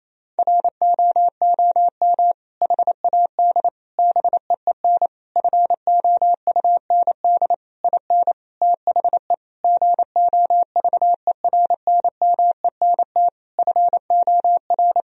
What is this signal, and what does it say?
Morse code, heard as a keyed tone, about 28 wpm, 700 Hz. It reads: ROOM HAD BEEN FOUND IN THE GOVERNMENT FOR
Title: Sentences from Top 1000 Words - Review Part 3 - 28wpm